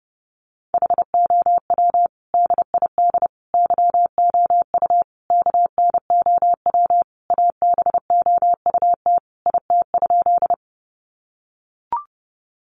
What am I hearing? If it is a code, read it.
Morse code, heard as a keyed tone, about 30 wpm, 700 Hz. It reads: HOW DID YOU KNOW ABOUT IT?
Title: Sentences from Top 100 Words 30wpm